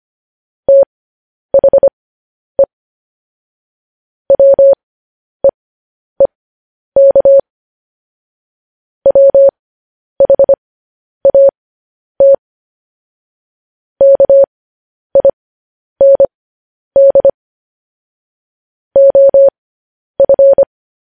Morse code, heard as a keyed tone, about 25 wpm, spaced as for 10 wpm, 550 Hz. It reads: THE WEEK WHAT KIND OF
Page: news headlines in morse code at 10 wpm